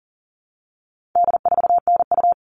Morse code, heard as a keyed tone, about 40 wpm, 700 Hz. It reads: D4NU
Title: All Cut Numbers - 4-Digit 40wpm